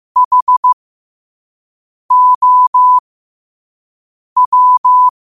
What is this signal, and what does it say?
Morse code, heard as a keyed tone, about 15 wpm, spaced as for 5 wpm, 1000 Hz. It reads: HOW
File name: qotdfiles / 5wpm